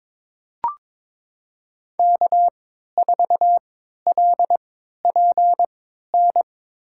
Morse code, heard as a keyed tone, about 22 wpm, spaced as for 13 wpm, 700 Hz. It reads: K4LPN
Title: USA Call Signs - 5 Characters - 3x Character Spacing - 22wpm